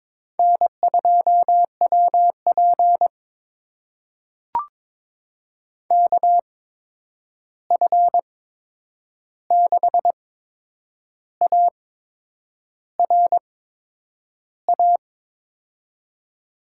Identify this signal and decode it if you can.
Morse code, heard as a keyed tone, about 22 wpm, spaced as for 8 wpm, 700 Hz. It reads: N2WP KF6ARA
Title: USA Call Signs - 8x Character Spacing - 22wpm